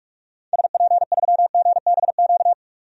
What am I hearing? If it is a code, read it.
Morse code, heard as a keyed tone, about 46 wpm, 700 Hz. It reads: SP3GBQ
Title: DXCC-100 Call Signs - 6 Characters - 45wpm